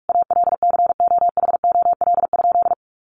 Morse code, heard as a keyed tone, about 44 wpm, 700 Hz. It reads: ALCOHOL?
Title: Sentences from Top 1400 Words 45wpm